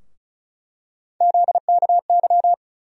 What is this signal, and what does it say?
Morse code, heard as a keyed tone, about 34 wpm, 700 Hz. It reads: ZKY
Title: USA Call Sign Postfixes - 3 Letters 35wpm